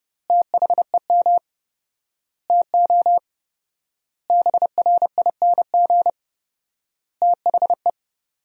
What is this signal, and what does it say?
Morse code, heard as a keyed tone, about 30 wpm, 700 Hz. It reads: THEM TO BRING THE
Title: 4x Word-Spacing - Sentences from Top 500 Words - 30wpm